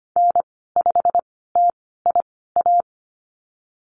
Morse code, heard as a keyed tone, about 25 wpm, spaced as for 16 wpm, 700 Hz. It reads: N5TIA
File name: Lesson 15g - Number 5 - Call Signs 25wpm